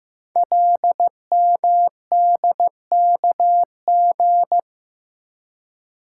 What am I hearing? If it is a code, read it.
Morse code, heard as a keyed tone, about 15 wpm, 700 Hz. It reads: LMDKG
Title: Code Groups - 5 Letters - 15wpm